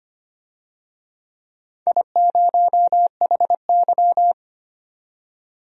Morse code, heard as a keyed tone, about 25 wpm, 700 Hz. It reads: I0HY
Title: DXCC-100 Call Signs - 4 Characters - 25wpm